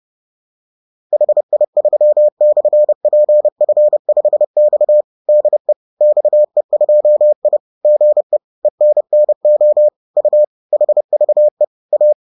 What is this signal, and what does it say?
Morse code, heard as a keyed tone, about 30 wpm, 600 Hz. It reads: HI3/PF5X DE XE2I GE ENNO U HVE A